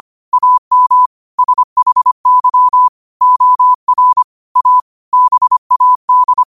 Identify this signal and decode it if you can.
Morse code, heard as a keyed tone, about 25 wpm, 1000 Hz. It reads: AM SHY OR A BAD
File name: qotdfiles / 25wpm